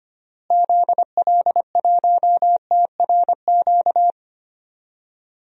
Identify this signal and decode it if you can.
Morse code, heard as a keyed tone, about 25 wpm, 700 Hz. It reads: ZL1TRQ
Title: DXCC-100 Call Signs - 6 Characters - Repeated 2x - 25wpm